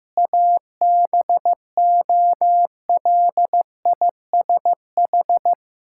Morse code, heard as a keyed tone, about 15 wpm, 700 Hz. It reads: ABOLISH
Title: Pangram Challenge v2 - 15wpm